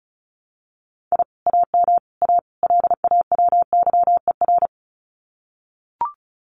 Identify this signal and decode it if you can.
Morse code, heard as a keyed tone, about 36 wpm, 700 Hz. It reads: I AM A LAWYER
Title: Common QSO Phrases Encore 35wpm